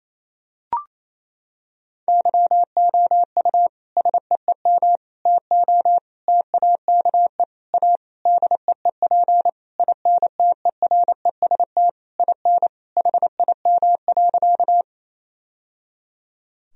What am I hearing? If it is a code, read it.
Morse code, heard as a keyed tone, about 28 wpm, 700 Hz. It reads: YOU SEEM TO TAKE A DEEP INTEREST IN HIM.
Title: Sentences from Top 1000 Words - Review Part 3 - 28wpm